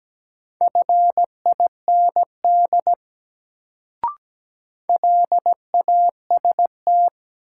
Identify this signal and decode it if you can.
Morse code, heard as a keyed tone, about 17 wpm, 700 Hz. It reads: FIND LAST
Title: Sets of 2 Words - New Word Drill - 17wpm